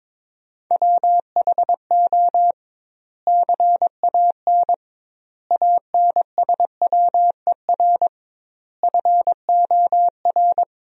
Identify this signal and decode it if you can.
Morse code, heard as a keyed tone, about 21 wpm, 700 Hz. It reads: WHO CAN ANSWER FOR
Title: 2x Word-Spacing - Sentences from Top 300 Words - 22wpm